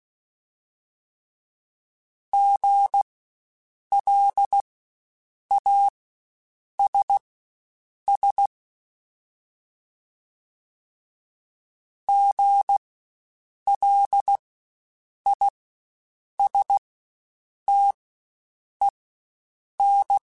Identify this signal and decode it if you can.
Morse code, heard as a keyed tone, about 16 wpm, spaced as for 7 wpm, 800 Hz. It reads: GLASS GLISTEN
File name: Spaced Alphas / Spaced Alphas mp3s / Gs sp @15